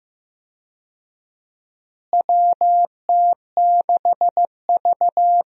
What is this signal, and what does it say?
Morse code, heard as a keyed tone, about 15 wpm, 700 Hz. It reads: WT6V
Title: USA Call Signs - 4 Characters - Repeated 3x - 15wpm